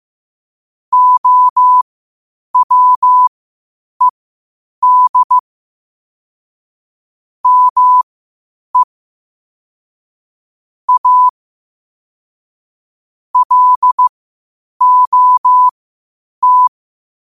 Morse code, heard as a keyed tone, about 15 wpm, spaced as for 8 wpm, 1000 Hz. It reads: OWED ME A LOT
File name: qotdfiles / 8wpm